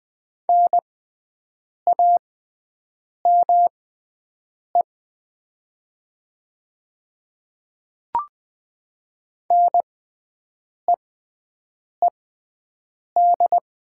Morse code, heard as a keyed tone, about 20 wpm, spaced as for 7 wpm, 700 Hz. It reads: NAME NEED
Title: 6x Character Spacing - Top 200 Words - 20wpm